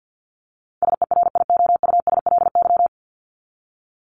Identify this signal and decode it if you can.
Morse code, heard as a keyed tone, about 55 wpm, 700 Hz. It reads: SERIOUSLY
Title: LY Suffix - Top 100 Words with Base Words 50wpm